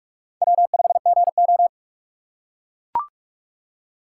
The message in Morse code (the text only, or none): W5GO